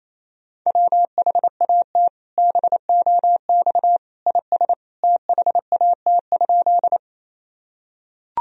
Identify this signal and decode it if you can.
Morse code, heard as a keyed tone, about 28 wpm, 700 Hz. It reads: WHAT BOX IS THAT?
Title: Sentences from Top 1000 Words - Review Part 2 - 28wpm